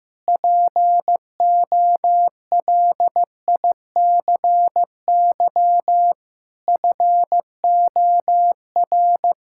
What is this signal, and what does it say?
Morse code, heard as a keyed tone, about 15 wpm, 700 Hz. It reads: POLICY FOR
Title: Sentences from Top 1000 Words - Review Part 5 - 15wpm